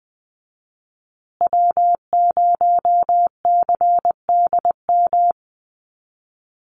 Morse code, heard as a keyed tone, about 20 wpm, 700 Hz. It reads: W0CDM
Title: USA Call Signs Repeated 3x 20wpm